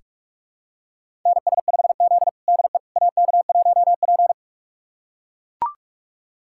Morse code, heard as a keyed tone, about 41 wpm, 700 Hz. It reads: NS5Z DE AK1P
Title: USA Callsign DE Callsign - 4 Characters - 45wpm